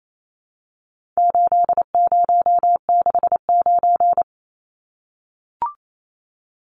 Morse code, heard as a keyed tone, about 28 wpm, 700 Hz. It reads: 8069
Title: Numbers - 4-Digit 28wpm